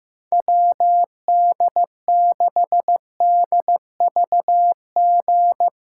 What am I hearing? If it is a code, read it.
Morse code, heard as a keyed tone, about 15 wpm, 700 Hz. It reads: WD6DVG
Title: USA Call Signs - 6 Characters - 3x Character Spacing - 15wpm